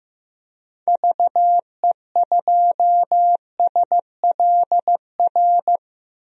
Morse code, heard as a keyed tone, about 15 wpm, 700 Hz. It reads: VE2SLR
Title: DXCC-100 Call Signs - 6 Characters - 15wpm